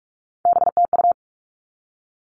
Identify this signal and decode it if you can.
Morse code, heard as a keyed tone, about 44 wpm, 700 Hz. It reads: BTU
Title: All Cut Numbers - 3-Digit 45wpm